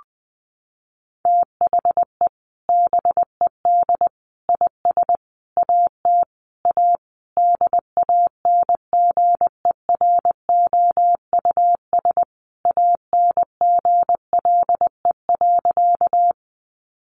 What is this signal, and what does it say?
Morse code, heard as a keyed tone, about 20 wpm, 700 Hz. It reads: THE BED IS AT A DANGEROUS ANGLE.
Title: Sentences from Top 1500 Words 20wpm